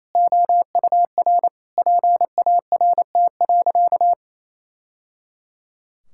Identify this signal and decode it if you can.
Morse code, heard as a keyed tone, about 28 wpm, 700 Hz. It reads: OUR PART.
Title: Sentences from Top 1000 Words - Review Part 6 - 28wpm